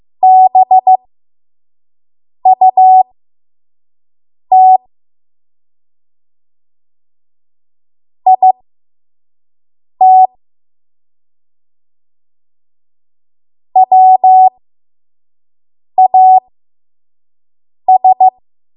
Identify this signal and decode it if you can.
Morse code, heard as a keyed tone, about 15 wpm, spaced as for 5 wpm, 750 Hz. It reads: BUT IT WAS
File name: Archive / 5 WPM / 05WPM